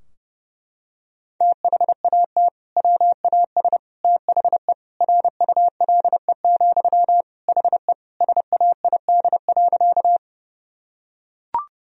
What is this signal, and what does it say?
Morse code, heard as a keyed tone, about 32 wpm, 700 Hz. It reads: THAT WAS THE RULE, HE SAID.
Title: Sentences from Top 1000 Words - Review Part 2 - 30wpm